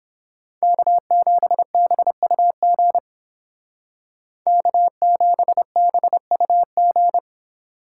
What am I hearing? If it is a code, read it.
Morse code, heard as a keyed tone, about 30 wpm, 700 Hz. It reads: K7BUG K7BUG